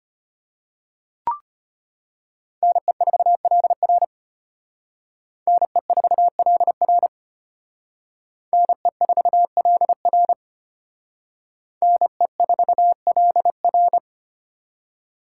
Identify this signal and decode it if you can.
Morse code, heard as a keyed tone, about 32 wpm, 700 Hz. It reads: NE4LR NE4LR NE4LR NE4LR